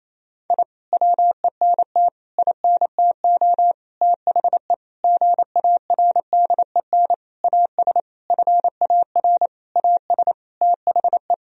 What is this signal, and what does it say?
Morse code, heard as a keyed tone, about 27 wpm, 700 Hz. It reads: I WENT INTO THE GARDEN AS FAR AS THE
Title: Sentences from Top 1000 Words - Review Part 4 - 28wpm